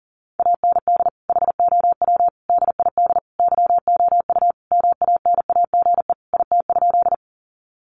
Morse code, heard as a keyed tone, about 40 wpm, 700 Hz. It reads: AND HOW DID YOU MANAGE IT?
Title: Sentences from Top 1400 Words 40wpm